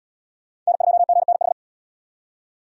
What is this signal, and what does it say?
Morse code, heard as a keyed tone, about 65 wpm, 700 Hz. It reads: N2GNH